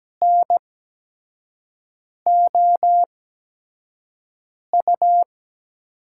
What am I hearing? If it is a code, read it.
Morse code, heard as a keyed tone, about 17 wpm, spaced as for 5 wpm, 700 Hz. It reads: NOU
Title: USA Call Signs - 8x Character Spacing - 17wpm